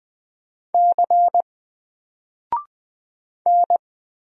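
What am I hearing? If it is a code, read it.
Morse code, heard as a keyed tone, about 20 wpm, 700 Hz. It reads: C N
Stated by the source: Single letter and number Morse drill at 20 wpm, from N4LCD.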